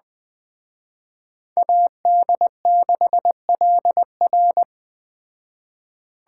Morse code, heard as a keyed tone, about 21 wpm, 700 Hz. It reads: AD6LR
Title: Speed Racing - USA 2x2 Call Signs 17wpm